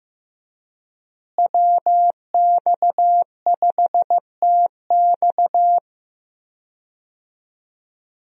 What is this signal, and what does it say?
Morse code, heard as a keyed tone, about 15 wpm, 700 Hz. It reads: WX5TX
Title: USA Call Signs - 5 Characters - Repeated 3x - 15wpm